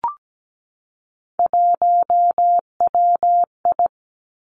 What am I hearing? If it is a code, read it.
Morse code, heard as a keyed tone, about 17 wpm, 700 Hz. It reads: 1WI
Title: Three Challenging Letter-Number Combinations 17wpm